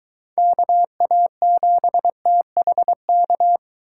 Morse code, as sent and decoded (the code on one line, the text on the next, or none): -.- .- --... - .... -.-
KA7THK